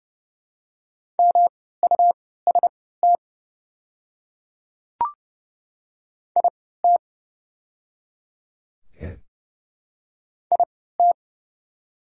Morse code, heard as a keyed tone, about 30 wpm, spaced as for 17 wpm, 700 Hz. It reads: MUST IT IT
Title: 3x Character Spacing - Top 200 Words - 30wpm